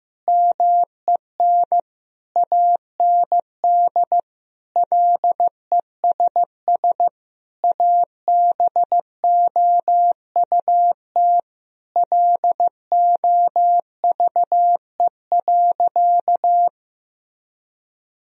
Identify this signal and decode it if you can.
Morse code, heard as a keyed tone, about 15 wpm, 700 Hz. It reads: MEN AND LESS ABOUT LOVE.
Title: Sentences from Top 1000 Words - Review Part 2 - 15wpm